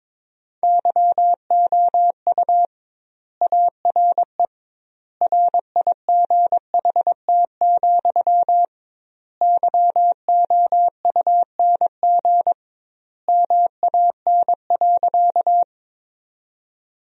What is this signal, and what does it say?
Morse code, heard as a keyed tone, about 22 wpm, 700 Hz. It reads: YOU ARE RIGHT, YOUNG MAN.
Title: 2x Word-Spacing - Sentences from Top 400 Words - 22wpm